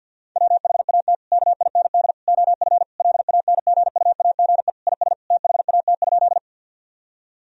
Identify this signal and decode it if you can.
Morse code, heard as a keyed tone, about 50 wpm, 700 Hz. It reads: WHAT KIND OF LANGUAGE IS THAT?
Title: Sentences from Top 1000 Words - Review Part 3 - 50wpm